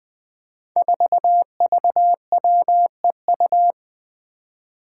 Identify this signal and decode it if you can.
Morse code, heard as a keyed tone, about 20 wpm, 700 Hz. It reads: 4VWEU
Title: Five Challenging Letter-Number Combinations 20wpm